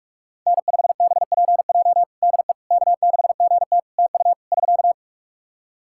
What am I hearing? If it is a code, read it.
Morse code, heard as a keyed tone, about 46 wpm, 700 Hz. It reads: N5BPJ DE K6GT TU <SK>
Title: K1USN SST QSO Practice 45wpm